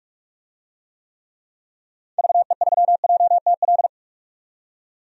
Morse code, heard as a keyed tone, about 42 wpm, 700 Hz. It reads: VE3JTL
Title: DXCC-100 Call Signs - 6 Characters - 45wpm